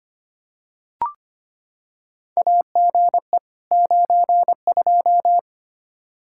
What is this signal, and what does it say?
Morse code, heard as a keyed tone, about 25 wpm, 700 Hz. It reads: AGE 92
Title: Practice QSO - Exchange 3 - More Info - 25wpm